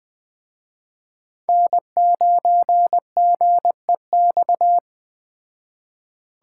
Morse code, heard as a keyed tone, about 20 wpm, 700 Hz. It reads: N9GEX
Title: Speed Racing - USA 1x3 Call Signs 17wpm